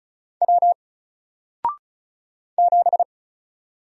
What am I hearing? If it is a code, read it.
Morse code, heard as a keyed tone, about 35 wpm, 700 Hz. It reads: W 7
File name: Single-Letter-Number-35wpm